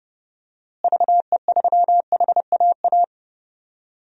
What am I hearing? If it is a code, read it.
Morse code, heard as a keyed tone, about 32 wpm, 700 Hz. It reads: VE3HAA
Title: DXCC-100 Call Signs - 6 Characters - Repeated 2x - 30wpm